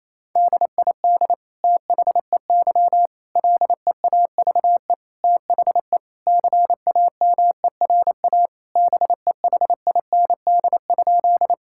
Morse code, heard as a keyed tone, about 28 wpm, 700 Hz. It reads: DID THEY LEAVE THE CAMERA BEHIND?
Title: Sentences from Top 1000 Words - Review Part 5 - 28wpm